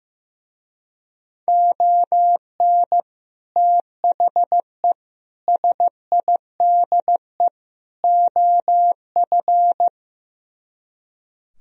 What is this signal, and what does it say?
Morse code, heard as a keyed tone, about 17 wpm, 700 Hz. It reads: ON THE SIDE OF
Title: Speed-Racing - Sets of 4 Words 15wpm